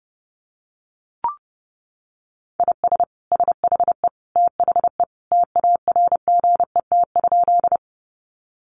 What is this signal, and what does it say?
Morse code, heard as a keyed tone, about 30 wpm, 700 Hz. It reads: IS SHE THE TARGET?
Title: Sentences from Top 1200 Words 30wpm